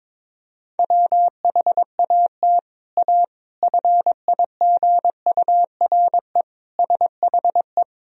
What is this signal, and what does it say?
Morse code, heard as a keyed tone, about 22 wpm, 700 Hz. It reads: WHAT A FIGURE SHE
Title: Sentences from Top 1000 Words - Review Part 2 - 22wpm